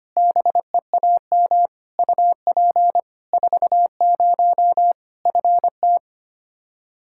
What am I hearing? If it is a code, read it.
Morse code, heard as a keyed tone, about 25 wpm, 700 Hz. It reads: BEAM UP 40 FT
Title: Practice QSO - Exchange 2 - Info - 25wpm